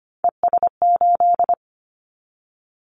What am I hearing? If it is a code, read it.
Morse code, heard as a keyed tone, about 25 wpm, 690 Hz. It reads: ES8
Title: Callsign Prefixes for Top 50 DXCC - 25wpm